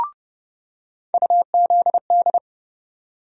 Morse code, heard as a keyed tone, about 30 wpm, 700 Hz. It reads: UZD
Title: USA Call Sign Postfixes - 3 Letters 30wpm